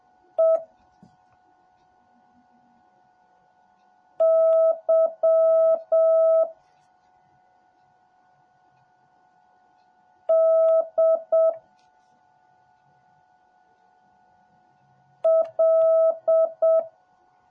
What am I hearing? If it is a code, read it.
Morse code, heard as a keyed tone, about 7 wpm, 650 Hz. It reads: E Y D L